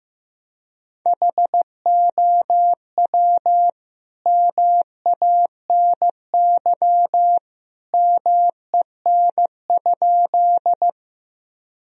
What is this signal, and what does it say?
Morse code, heard as a keyed tone, about 15 wpm, 700 Hz. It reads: HOW MANY MEN?
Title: Sentences from Top 1000 Words - Review Part 1 - 15wpm